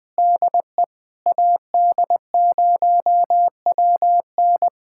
Morse code, heard as a keyed tone, about 20 wpm, 690 Hz. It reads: DE AD0WN
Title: POTA QSO Practice - 20wpm